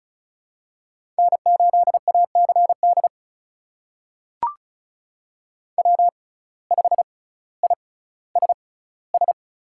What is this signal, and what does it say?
Morse code, heard as a keyed tone, about 36 wpm, spaced as for 18 wpm, 700 Hz. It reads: N8ACD W5ISS
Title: USA Call Signs - 5 Characters - 6x Character Spacing - 35wpm